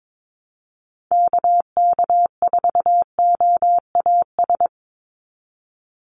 Morse code, heard as a keyed tone, about 22 wpm, 700 Hz. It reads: KK4OAS
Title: USA Call Signs 22wpm